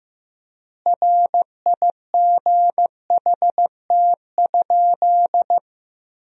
Morse code, heard as a keyed tone, about 15 wpm, 700 Hz. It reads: RIGHT?
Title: 3x Word-Spacing - Sentences from Top 200 Words - 15wpm